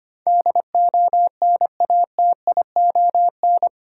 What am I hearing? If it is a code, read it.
Morse code, heard as a keyed tone, about 25 wpm, 700 Hz. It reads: DONATION
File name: 8 Letter Words - First 3 Letters Given - 25wpm